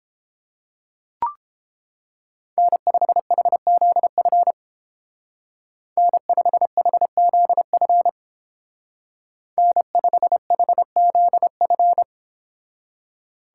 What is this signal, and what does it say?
Morse code, heard as a keyed tone, about 31 wpm, 700 Hz. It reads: N5HZF N5HZF N5HZF